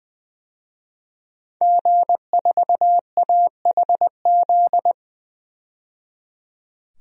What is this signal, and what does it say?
Morse code, heard as a keyed tone, about 21 wpm, 700 Hz. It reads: G4AHZ
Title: DXCC-100 Call Signs - 5 Characters - 20wpm